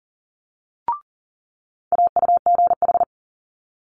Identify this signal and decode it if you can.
Morse code, heard as a keyed tone, about 38 wpm, 700 Hz. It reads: AUGH
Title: Multi-letter Phonemes Quick Fire 40wpm